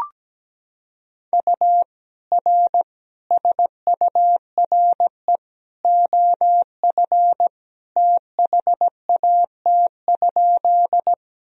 Morse code, heard as a keyed tone, about 17 wpm, 700 Hz. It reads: U R SURE OF THAT?
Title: Sentences from Top 300 Words and Abbreviations 17wpm